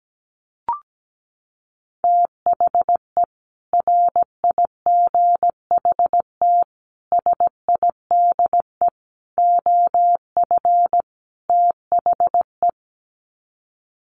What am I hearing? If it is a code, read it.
Morse code, heard as a keyed tone, about 17 wpm, 700 Hz. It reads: THE RIGHT SIDE OF THE
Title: Sets of 5 Words Encore 17wpm